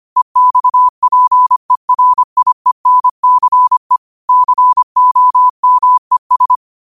Morse code, heard as a keyed tone, about 25 wpm, 1000 Hz. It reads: EXPERIENCE COMES